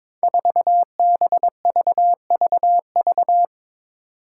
Morse code, heard as a keyed tone, about 22 wpm, 700 Hz. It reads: 4BVVV
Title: Five Challenging Letter-Number Combinations 22wpm